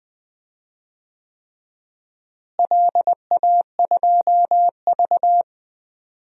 Morse code, heard as a keyed tone, about 20 wpm, 700 Hz. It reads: LA2V
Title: DXCC-100 Call Signs - 4 Characters - Repeated 3x - 20wpm